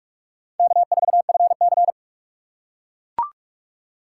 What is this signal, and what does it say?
Morse code, heard as a keyed tone, about 44 wpm, 700 Hz. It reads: K4FC